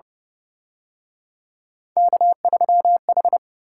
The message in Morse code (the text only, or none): K3H